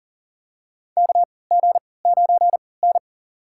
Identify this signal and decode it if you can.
Morse code, heard as a keyed tone, about 40 wpm, spaced as for 23 wpm, 700 Hz. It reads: KG9N
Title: USA Call Signs - 3x Character Spacing - 40wpm